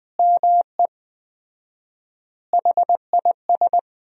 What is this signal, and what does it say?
Morse code, heard as a keyed tone, about 20 wpm, 700 Hz. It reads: ME HIS
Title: 4x Word-Spacing - Sentences from Top 200 Words - 20wpm